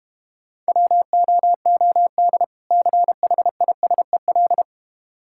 Morse code, heard as a keyed tone, about 34 wpm, 700 Hz. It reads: WOOD CHISEL